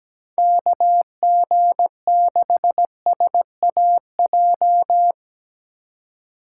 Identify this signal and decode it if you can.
Morse code, heard as a keyed tone, about 19 wpm, 700 Hz. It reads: KG6SAJ
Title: Speed Racing - USA 2x3 Call Signs 17wpm